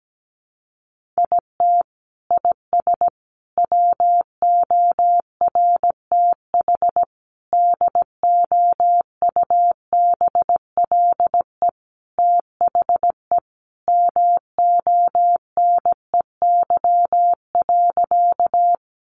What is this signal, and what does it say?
Morse code, heard as a keyed tone, about 17 wpm, 700 Hz. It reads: IT IS WORTH DOUBLE THE MONEY.
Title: Sentences from Top 1400 Words 17wpm